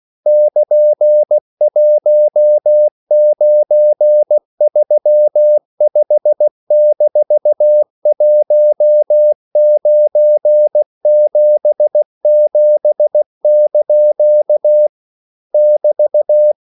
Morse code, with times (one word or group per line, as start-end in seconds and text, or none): 0.3-14.9 <KN>1935-1977)
15.5-16.5 <BT>